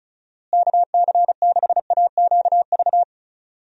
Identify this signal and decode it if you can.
Morse code, heard as a keyed tone, about 34 wpm, 700 Hz. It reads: KC6AQV
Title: USA Call Signs - 6 Characters - Repeated 2x - 35wpm